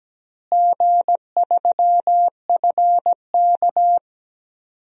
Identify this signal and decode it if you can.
Morse code, heard as a keyed tone, about 17 wpm, 700 Hz. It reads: G3FK
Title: DXCC-100 Call Signs - 4 Characters - Repeated 2x - 17wpm